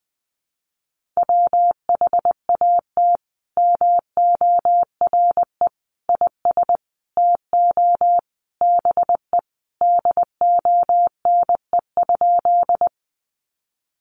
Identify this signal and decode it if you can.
Morse code, heard as a keyed tone, about 20 wpm, 700 Hz. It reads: WHAT MORE IS TO BE DONE?
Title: Sentences from Top 500 Words 20wpm